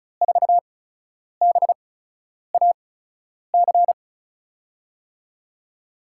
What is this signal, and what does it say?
Morse code, heard as a keyed tone, about 36 wpm, spaced as for 10 wpm, 700 Hz. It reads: 4BAC
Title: USA Call Signs - 6 Characters - 8x Character Spacing - 35wpm